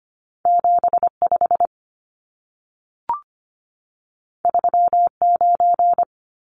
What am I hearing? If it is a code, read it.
Morse code, heard as a keyed tone, about 25 wpm, 700 Hz. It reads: 75 39
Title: Numbers - 2-Digit 25wpm